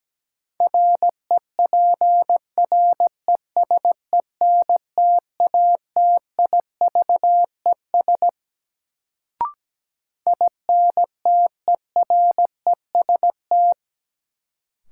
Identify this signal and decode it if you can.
Morse code, heard as a keyed tone, about 17 wpm, 700 Hz. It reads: REPRESENTATIVES INTEREST
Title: Sets of 5 Words Encore - New Word Drill - 17wpm